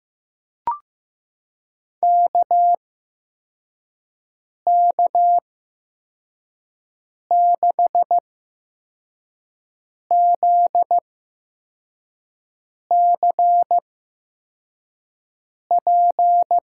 Morse code, heard as a keyed tone, about 15 wpm, spaced as for 4 wpm, 700 Hz. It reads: KK6ZCP
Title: USA Call Signs - 8x Character Spacing - 15wpm